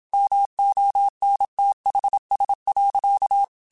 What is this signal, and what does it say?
Morse code, heard as a keyed tone, about 27 wpm, 800 Hz. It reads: MONTHS.
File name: QSOs / QSOs-mp3s / WD0FBT 1@25